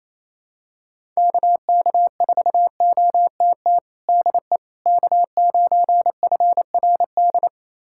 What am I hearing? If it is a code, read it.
Morse code, heard as a keyed tone, about 28 wpm, 700 Hz. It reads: KK4OTT DE K9FRD
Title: USA Callsign DE Callsign 28wpm